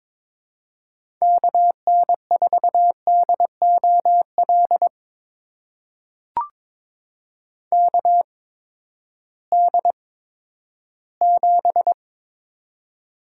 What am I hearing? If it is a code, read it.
Morse code, heard as a keyed tone, about 22 wpm, spaced as for 13 wpm, 700 Hz. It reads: KN4DOL KD7
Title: USA Call Signs - 8x Character Spacing - 22wpm